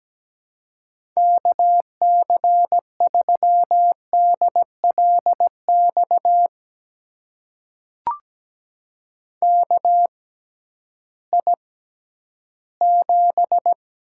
Morse code, heard as a keyed tone, about 17 wpm, spaced as for 11 wpm, 700 Hz. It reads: KC3DLX KI7
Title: USA Call Signs - 6 Characters - 6x Character Spacing - 17wpm